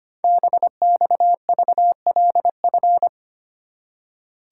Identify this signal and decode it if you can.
Morse code, heard as a keyed tone, about 25 wpm, 700 Hz. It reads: BXVLF
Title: Code Groups - 5 Letters - 25wpm